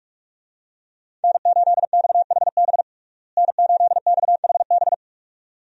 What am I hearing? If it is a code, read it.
Morse code, heard as a keyed tone, about 46 wpm, 700 Hz. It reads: N8XHB N8XHB
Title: USA Call Signs - 5 Characters - Repeated 2x - 45wpm